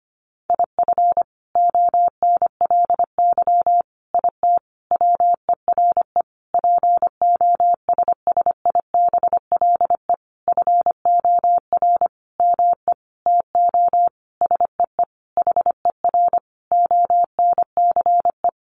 Morse code, heard as a keyed tone, about 25 wpm, 700 Hz. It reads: IF ONLY IT WERE POSSIBLE FOR ME TO SEE HER ONCE
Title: Sentences from Top 500 Words 25wpm